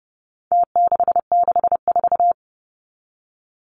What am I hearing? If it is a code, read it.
Morse code, heard as a keyed tone, about 30 wpm, 700 Hz. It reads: T664
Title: Numbers - Contest Style - 4-Digit 30wpm